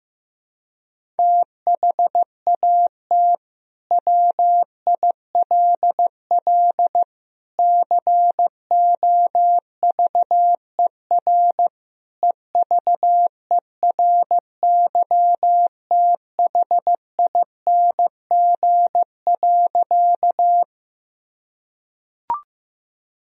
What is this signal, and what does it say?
Morse code, heard as a keyed tone, about 15 wpm, 700 Hz. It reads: THAT WILL COVER EVERYTHING.
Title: Sentences from Top 1000 Words - Review Part 1 - 15wpm